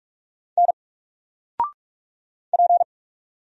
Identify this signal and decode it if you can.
Morse code, heard as a keyed tone, about 46 wpm, 700 Hz. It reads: N P
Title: Single Letter-Number 45wpm